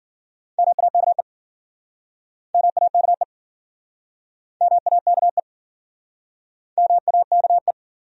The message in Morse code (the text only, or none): MAKE MAKE MAKE MAKE